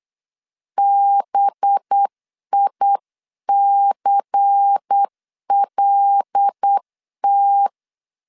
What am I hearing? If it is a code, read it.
Morse code, heard as a keyed tone, about 8 wpm, 800 Hz. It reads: BICLT
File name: CW / DET / 1 6 1500C S C 8wpm 5219